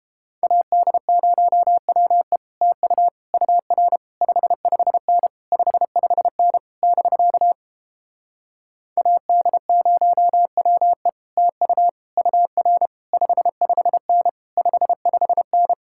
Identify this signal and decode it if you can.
Morse code, heard as a keyed tone, about 29 wpm, 700 Hz. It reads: AD0WE TU UR 55N 55N <BK> AD0WE TU UR 55N 55N